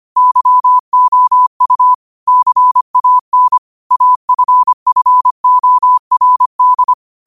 Morse code, heard as a keyed tone, about 25 wpm, 1000 Hz. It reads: YOU CAN AFFORD
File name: qotdfiles / 25wpm